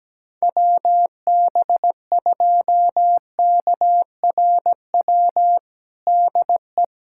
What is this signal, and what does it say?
Morse code, heard as a keyed tone, about 17 wpm, 700 Hz. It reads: WB2KRW DE
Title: USA Callsign DE Callsign - 6 Characters - 17wpm